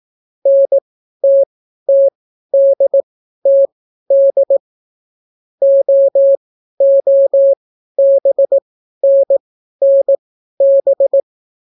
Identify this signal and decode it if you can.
Morse code, heard as a keyed tone, about 18 wpm, spaced as for 12 wpm, 550 Hz. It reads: NTTDTD OOBNNB